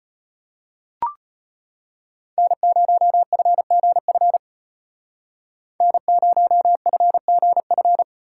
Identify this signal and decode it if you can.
Morse code, heard as a keyed tone, about 34 wpm, 700 Hz. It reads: N0FGF N0FGF